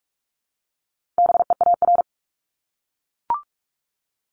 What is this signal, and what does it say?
Morse code, heard as a keyed tone, about 44 wpm, 700 Hz. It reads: BEAR
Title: ING Suffix - Top 100 Words with Base Words 45wpm